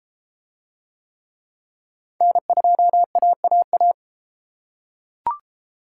Morse code, heard as a keyed tone, about 32 wpm, 700 Hz. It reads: N2AAA